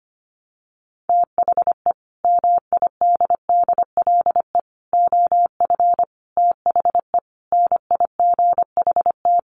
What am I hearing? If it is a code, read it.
Morse code, heard as a keyed tone, about 25 wpm, 700 Hz. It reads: THE MIDDLE OF THE NIGHT
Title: Sets of 5 Words 25wpm